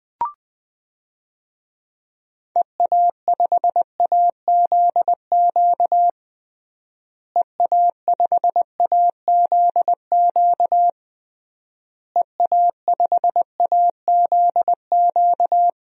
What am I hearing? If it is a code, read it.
Morse code, heard as a keyed tone, about 20 wpm, 700 Hz. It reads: EA5AZQ EA5AZQ EA5AZQ
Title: DXCC-100 Call Signs - 6 Characters - Repeated 3x - 20wpm